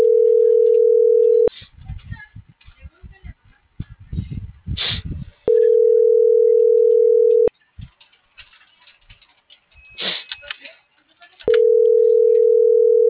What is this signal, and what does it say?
Ringback tone (North America), 440 + 480 Hz.